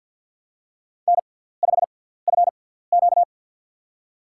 Morse code, heard as a keyed tone, about 50 wpm, spaced as for 17 wpm, 700 Hz. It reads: N5FQ
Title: USA Call Signs - 4 Characters - 6x Character Spacing - 50wpm